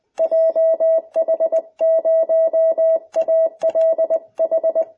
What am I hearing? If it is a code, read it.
Morse code, heard as a keyed tone, about 20 wpm, 650 Hz. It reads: JH0ALH